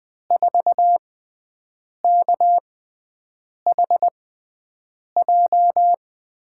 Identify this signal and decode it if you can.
Morse code, heard as a keyed tone, about 20 wpm, spaced as for 7 wpm, 700 Hz. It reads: 4KHJ
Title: USA Call Signs - 6x Character Spacing - 20wpm